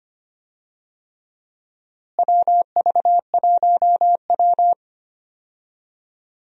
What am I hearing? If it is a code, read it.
Morse code, heard as a keyed tone, about 25 wpm, 700 Hz. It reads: WV1W
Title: USA Call Signs - 4 Characters - Repeated 3x - 25wpm